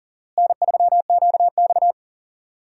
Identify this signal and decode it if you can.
Morse code, heard as a keyed tone, about 40 wpm, 700 Hz. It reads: N3QX
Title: USA Call Signs - 4 Characters - Repeated 3x - 40wpm